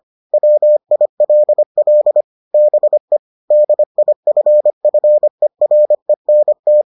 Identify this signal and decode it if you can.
Morse code, heard as a keyed tone, about 25 wpm, 600 Hz. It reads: WILL BE DIFFERENT